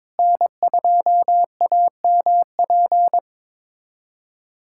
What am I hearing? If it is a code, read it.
Morse code, heard as a keyed tone, about 22 wpm, 700 Hz. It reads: N2AMP
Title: USA Callsign DE Callsign - 5 Characters - 22wpm